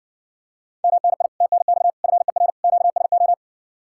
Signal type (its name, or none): Morse code